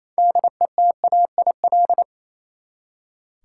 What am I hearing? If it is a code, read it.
Morse code, heard as a keyed tone, about 29 wpm, 700 Hz. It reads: DETAIL
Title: Six Letter Words - 28wpm